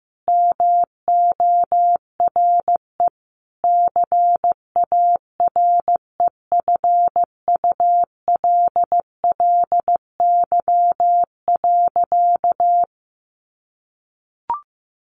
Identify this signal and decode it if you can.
Morse code, heard as a keyed tone, about 15 wpm, 700 Hz. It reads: MORE CAREFULLY.
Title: Sentences from Top 1300 Words 15wpm